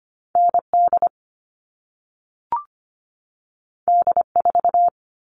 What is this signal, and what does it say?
Morse code, heard as a keyed tone, about 25 wpm, 700 Hz. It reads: ND D4
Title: All Cut Numbers - 2-Digit 25wpm